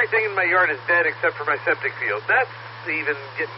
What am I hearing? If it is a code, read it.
Emergency Alert System (SAME) tones, heard as harsh buzzing bursts, with speech. It is heard from 0.4 to 3.2 s.